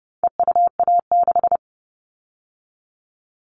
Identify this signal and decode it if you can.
Morse code, heard as a keyed tone, about 30 wpm, 700 Hz. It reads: EUA6
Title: All Cut Numbers - 4-Digit 30wpm